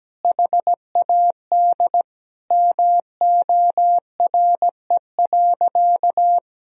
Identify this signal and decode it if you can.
Morse code, heard as a keyed tone, about 17 wpm, 700 Hz. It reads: HAD MORE.
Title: Sentences from Top 1000 Words - Review Part 1 - 17wpm